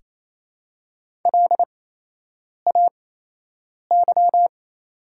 Morse code, heard as a keyed tone, about 28 wpm, spaced as for 8 wpm, 700 Hz. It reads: LAY
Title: 8x Character Spacing - Top 500 Words - 28wpm